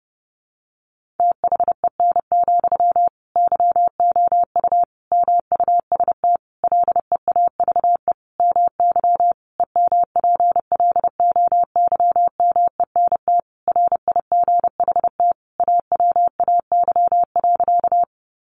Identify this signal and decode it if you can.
Morse code, heard as a keyed tone, about 28 wpm, 700 Hz. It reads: THEN, YOU MUST LEAVE MY EMPLOYMENT RIGHT AWAY.
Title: Sentences from Top 1500 Words 30wpm